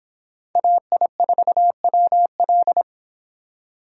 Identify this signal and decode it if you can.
Morse code, heard as a keyed tone, about 26 wpm, 700 Hz. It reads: AI4WL